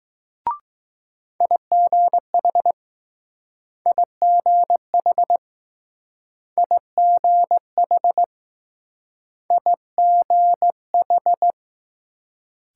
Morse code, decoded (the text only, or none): IGH IGH IGH IGH